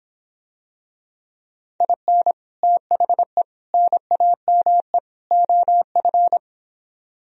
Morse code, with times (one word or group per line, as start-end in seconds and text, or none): 1.8-2.3 IN
2.6-3.4 THE
3.7-5.0 NAME
5.3-6.4 OF